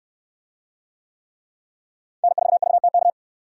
Morse code, heard as a keyed tone, about 66 wpm, 700 Hz. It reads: N4VTK